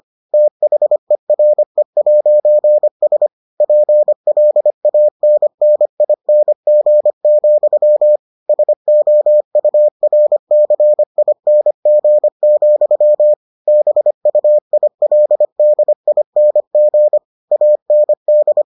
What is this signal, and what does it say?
Morse code, heard as a keyed tone, about 25 wpm, 600 Hz. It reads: THERE'S PLANNING, SOURCING, BUILDING AND